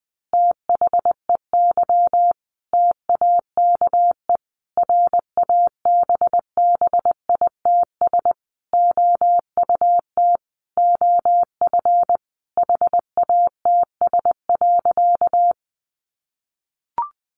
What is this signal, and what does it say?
Morse code, heard as a keyed tone, about 20 wpm, 700 Hz. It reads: THEY TAKE RABBITS OUT OF HATS.
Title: Easy Phrases 20wpm